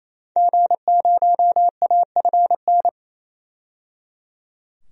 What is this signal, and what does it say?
Morse code, heard as a keyed tone, about 28 wpm, 700 Hz. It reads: G0AFN
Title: DXCC-100 Call Signs - 5 Characters - 28wpm